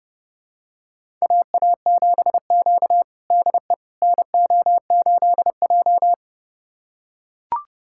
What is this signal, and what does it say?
Morse code, heard as a keyed tone, about 29 wpm, 700 Hz. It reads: AA7Q DE NO8J
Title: USA Callsign DE Callsign - 4 Characters - 30wpm